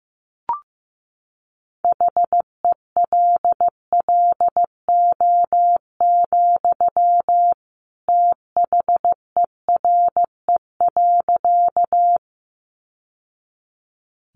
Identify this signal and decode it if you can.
Morse code, heard as a keyed tone, about 15 wpm, 700 Hz. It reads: HELLO, THERE.
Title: Sentences from Top 1100 Words 15wpm